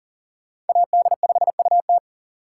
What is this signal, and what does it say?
Morse code, heard as a keyed tone, about 50 wpm, 700 Hz. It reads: AD5UT
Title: Speed Racing - USA 2x2 Call Signs 40wpm